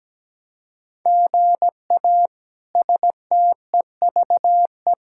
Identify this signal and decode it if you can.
Morse code, heard as a keyed tone, about 17 wpm, 700 Hz. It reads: GA STEVE
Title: K1USN SST QSO Practice - 17wpm